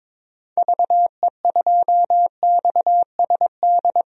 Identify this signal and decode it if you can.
Morse code, heard as a keyed tone, about 22 wpm, 700 Hz. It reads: VE2XSD
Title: DXCC-100 Call Signs - 6 Characters - 22wpm